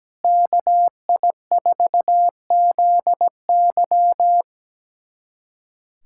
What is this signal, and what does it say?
Morse code, heard as a keyed tone, about 17 wpm, 700 Hz. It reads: KI4ZY
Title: USA Callsign DE Callsign - 5 Characters - 17wpm